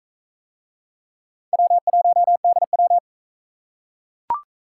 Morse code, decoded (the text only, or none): W1DW